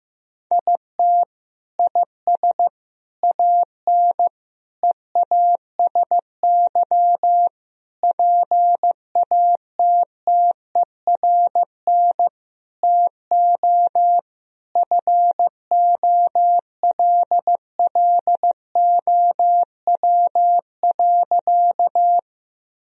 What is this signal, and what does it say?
Morse code, heard as a keyed tone, about 15 wpm, 700 Hz. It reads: IT IS AN EASY PATTERN TO FOLLOW.
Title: Sentences from Top 1000 Words - Review Part 2 - 15wpm